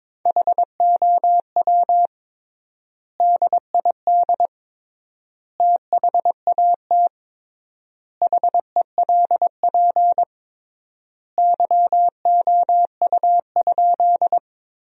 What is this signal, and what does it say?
Morse code, heard as a keyed tone, about 22 wpm, 700 Hz. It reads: HOW DID THAT HELP YOU?
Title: 3x Word-Spacing - Sentences from Top 200 Words - 22wpm